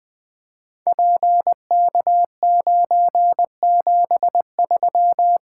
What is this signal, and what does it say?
Morse code, heard as a keyed tone, about 20 wpm, 700 Hz. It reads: PK973
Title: Code Groups - 5 Letter:Numbers - 20wpm